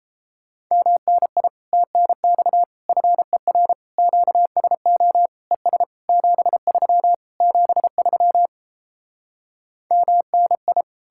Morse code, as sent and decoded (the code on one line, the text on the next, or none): -- -. .. / - -. -..- / ..-. .-. / --.- ... --- / . ... / --... ...-- / --... ...-- / -- -. ..
MNI TNX FER QSO ES 73 73 MNI